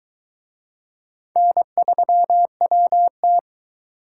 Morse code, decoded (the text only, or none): N3WT